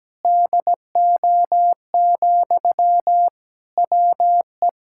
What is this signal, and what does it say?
Morse code, heard as a keyed tone, about 17 wpm, 700 Hz. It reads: DO, WE
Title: Sentences from Top 1600 Words 17wpm